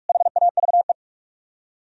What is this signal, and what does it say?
Morse code, heard as a keyed tone, about 58 wpm, 700 Hz. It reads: HAVE